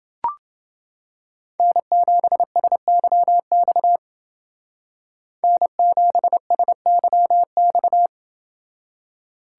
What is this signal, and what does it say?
Morse code, heard as a keyed tone, about 28 wpm, 700 Hz. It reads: N7SYX N7SYX